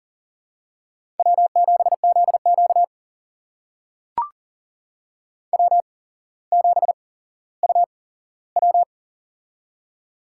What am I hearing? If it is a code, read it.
Morse code, heard as a keyed tone, about 41 wpm, spaced as for 17 wpm, 700 Hz. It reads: W7ZQ W7UW
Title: USA Call Signs - 4 Characters - 8x Character Spacing - 40wpm